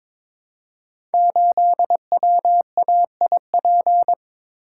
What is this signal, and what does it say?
Morse code, heard as a keyed tone, about 22 wpm, 700 Hz. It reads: 8WAIP
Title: Code Groups - 5 Letter:Numbers - 22wpm